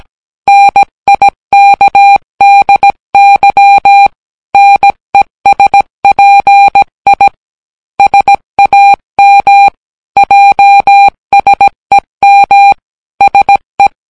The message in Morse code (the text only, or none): NIKDY NESPI SAM JSEM SE